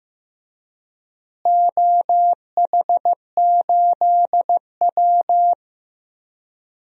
Morse code, heard as a keyed tone, about 15 wpm, 700 Hz. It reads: OH8W
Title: DXCC-100 Call Signs - 4 Characters - Repeated 2x - 15wpm